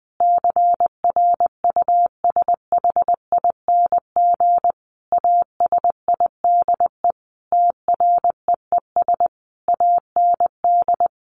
Morse code, heard as a keyed tone, about 20 wpm, 700 Hz. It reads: CRUSHING ASIDE TREES AND
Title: A Sound Of Thunder 20wpm